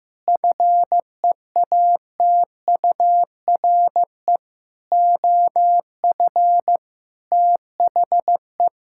Morse code, heard as a keyed tone, about 16 wpm, 700 Hz. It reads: FEATURE OF THE
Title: Sentences from Top 1000 Words - Review Part 5 - 15wpm